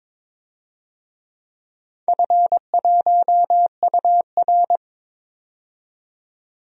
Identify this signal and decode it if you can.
Morse code, heard as a keyed tone, about 22 wpm, 700 Hz. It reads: F1UR
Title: DXCC-100 Call Signs - 4 Characters - 22wpm